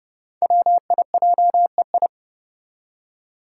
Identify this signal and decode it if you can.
Morse code, heard as a keyed tone, about 30 wpm, 700 Hz. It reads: WIJEI
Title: Five Challenging Letter-Number Combinations 30wpm